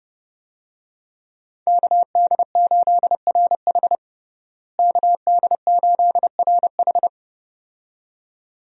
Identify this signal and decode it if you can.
Morse code, heard as a keyed tone, about 32 wpm, 700 Hz. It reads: KD8RH KD8RH
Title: USA Call Signs - 5 Characters - Repeated 2x - 30wpm